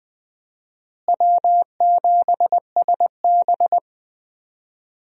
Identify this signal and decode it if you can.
Morse code, heard as a keyed tone, about 20 wpm, 700 Hz. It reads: W7SB
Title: USA Call Signs - 4 Characters - Repeated 2x - 20wpm